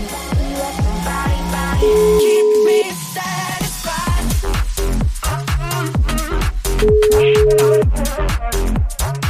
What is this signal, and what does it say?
Ringback tone (Europe, 425 Hz), with noise.